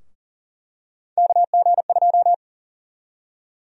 Morse code, heard as a keyed tone, about 40 wpm, 700 Hz. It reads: KG2
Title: USA Call Sign Prefixes - 40wpm